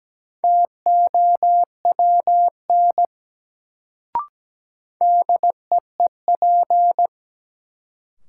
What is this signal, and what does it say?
Morse code, heard as a keyed tone, about 17 wpm, 700 Hz. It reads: TOWN DEEP
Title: Sentences from Top 500 Words - New Word Drill - 17wpm